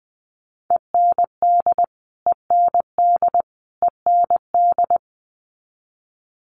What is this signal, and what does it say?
Morse code, heard as a keyed tone, about 20 wpm, 700 Hz. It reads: END END END
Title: Top 500 Words Repeated 20wpm